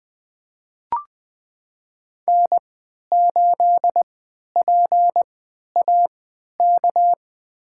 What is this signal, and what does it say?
Morse code, heard as a keyed tone, about 20 wpm, spaced as for 11 wpm, 700 Hz. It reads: N8PAK
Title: USA Call Signs - 5 Characters - 3x Character Spacing - 20wpm